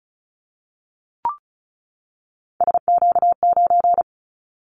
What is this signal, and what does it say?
Morse code, heard as a keyed tone, about 36 wpm, 700 Hz. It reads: SQ9
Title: Callsign Prefixes for Top 25 DXCC - 35wpm